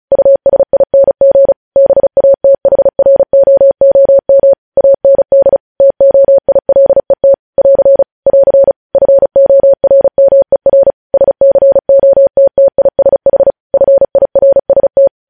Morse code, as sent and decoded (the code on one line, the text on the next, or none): ..- ... .. -. --. / -... .- - .... .-. --- --- -- / .- -. -.. / - --- .. .-.. - / .-.-. / .-.-. / ..-. --- .-. -- . .-. / ... -.-. --- - - .. ... .... / ..-. .. .-. ... -
USING BATHROOM AND TOILET <AR> <AR> FORMER SCOTTISH FIRST